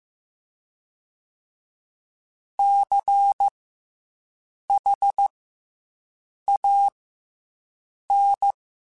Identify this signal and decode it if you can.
Morse code, heard as a keyed tone, about 15 wpm, spaced as for 6 wpm, 800 Hz. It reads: CHAN